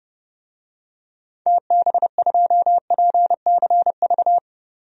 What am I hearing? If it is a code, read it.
Morse code, heard as a keyed tone, about 30 wpm, 700 Hz. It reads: TB2PCV
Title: DXCC-100 Call Signs - 6 Characters - 30wpm